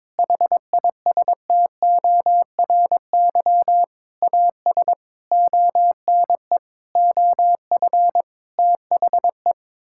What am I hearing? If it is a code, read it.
Morse code, heard as a keyed tone, about 22 wpm, 700 Hz. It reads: HISTORY AS ONE OF THE
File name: practice / 22wpm / Sets of 8 Words - 22wpm